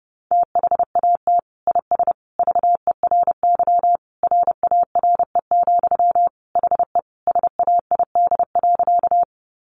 Morse code, heard as a keyed tone, about 30 wpm, 700 Hz. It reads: THAT IS VERY RARE, HE SAID.
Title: Sentences from Top 1100 Words 30wpm